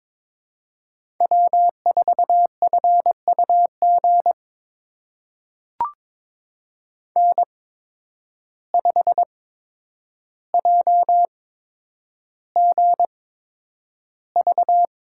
Morse code, heard as a keyed tone, about 23 wpm, spaced as for 10 wpm, 700 Hz. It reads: W4FUG N5JGV
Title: USA Call Signs - 5 Characters - 8x Character Spacing - 22wpm